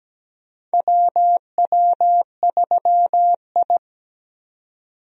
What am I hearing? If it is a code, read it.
Morse code, heard as a keyed tone, about 17 wpm, 700 Hz. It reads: WW3I
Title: USA Call Signs - 4 Characters - 17wpm